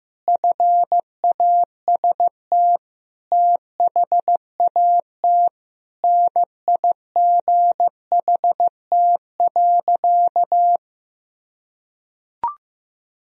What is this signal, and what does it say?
Morse code, heard as a keyed tone, about 15 wpm, 700 Hz. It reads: FAST THAT NIGHT.
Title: Sentences from Top 1000 Words - Review Part 2 - 15wpm